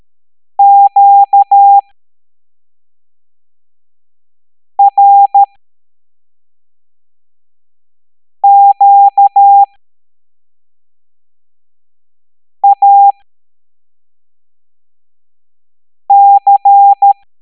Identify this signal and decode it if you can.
Morse code, heard as a keyed tone, about 13 wpm, spaced as for 3 wpm, 800 Hz. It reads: QRQAC